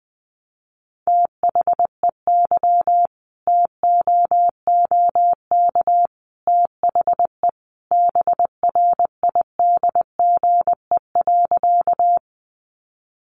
Morse code, heard as a keyed tone, about 20 wpm, 700 Hz. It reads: THEY TOOK THE BRIDGE.
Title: Sentences from Top 1200 Words 20wpm